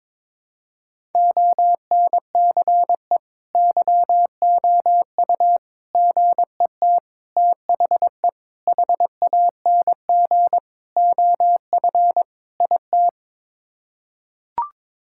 Morse code, heard as a keyed tone, about 21 wpm, 700 Hz. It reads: ONCE YOU GET THE HANG OF IT
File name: practice / 22wpm / Sets of 7 Words - 22wpm